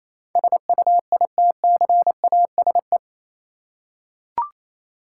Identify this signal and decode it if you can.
Morse code, heard as a keyed tone, about 27 wpm, 700 Hz. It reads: SUITCASE
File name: Easy Words - Speed-Racing with Hints - 22wpm